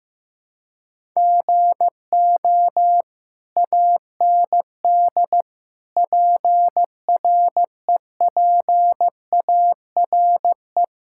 Morse code, heard as a keyed tone, about 15 wpm, 700 Hz. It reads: GO AND PREPARE
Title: Sentences from Top 1000 Words - Review Part 6 - 15wpm